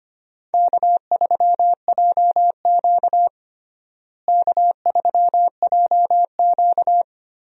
Morse code, heard as a keyed tone, about 25 wpm, 700 Hz. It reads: K3JQ K3JQ
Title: USA Call Signs - 4 Characters - Repeated 3x - 25wpm